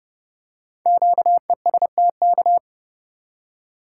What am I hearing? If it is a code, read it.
Morse code, heard as a keyed tone, about 30 wpm, 700 Hz. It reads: QESTK
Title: Code Groups - 5 Letters - 30wpm